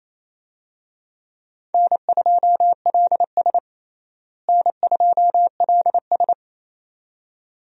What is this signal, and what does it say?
Morse code, heard as a keyed tone, about 27 wpm, 700 Hz. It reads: N2LS N2LS
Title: USA Call Signs - 4 Characters - Repeated 2x - 28wpm